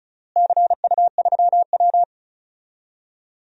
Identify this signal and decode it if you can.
Morse code, heard as a keyed tone, about 34 wpm, 700 Hz. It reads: CU3W
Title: DXCC-100 Call Signs - 4 Characters - Repeated 3x - 35wpm